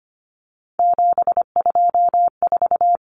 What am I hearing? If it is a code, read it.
Morse code, heard as a keyed tone, about 25 wpm, 700 Hz. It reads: 724
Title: Numbers - 3-Digit 25wpm